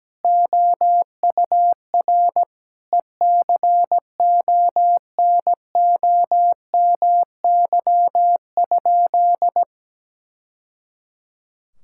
Morse code, heard as a keyed tone, about 17 wpm, 700 Hz. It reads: OUR ECONOMY?
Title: Sentences from Top 1000 Words - Review Part 5 - 17wpm